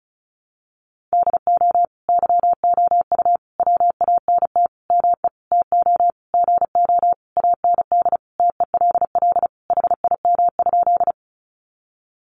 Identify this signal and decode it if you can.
Morse code, heard as a keyed tone, about 34 wpm, 700 Hz. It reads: DO YOU WANT ME TO GO AND TELL HIM?
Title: Sentences from Top 200 Words 35wpm